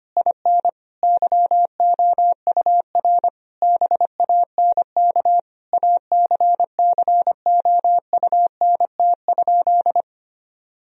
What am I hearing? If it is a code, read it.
Morse code, heard as a keyed tone, about 25 wpm, 700 Hz. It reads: IN YOUR BANK ACCOUNT?
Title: Sentences from Top 1000 Words - Review Part 4 - 25wpm